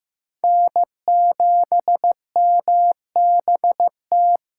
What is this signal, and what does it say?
Morse code, heard as a keyed tone, about 15 wpm, 700 Hz. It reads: N7MBT
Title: USA Callsign DE Callsign - 5 Characters - 15wpm